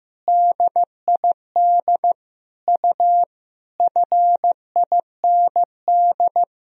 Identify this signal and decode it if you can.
Morse code, heard as a keyed tone, about 15 wpm, 700 Hz. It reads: DID U FIND
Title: Sentences from Top 200 Words and Abbreviations 15wpm